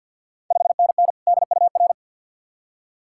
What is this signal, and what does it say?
Morse code, heard as a keyed tone, about 47 wpm, 690 Hz. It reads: 5NN DUR
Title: Ontario QSO Party Exchange 50wpm